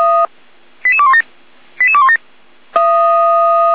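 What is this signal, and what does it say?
Five-tone selective call (ZVEI), heard as quick runs of five tones, tones of about 80 ms. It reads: 88217 88217
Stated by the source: ZVEI five-tone call 88199 received by SDR.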